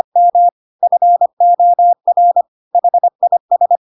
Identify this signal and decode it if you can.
Morse code, heard as a keyed tone, about 25 wpm, 700 Hz. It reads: M FOR HIS